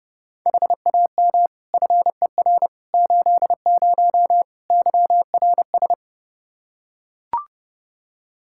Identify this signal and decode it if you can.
Morse code, heard as a keyed tone, about 32 wpm, 700 Hz. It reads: HAM FER 80 YRS
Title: Practice QSO - Exchange 3 - More Info - 30wpm